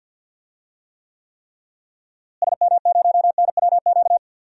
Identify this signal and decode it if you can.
Morse code, heard as a keyed tone, about 51 wpm, 700 Hz. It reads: SM0NWQ
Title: DXCC-100 Call Signs - 6 Characters - Repeated 3x - 50wpm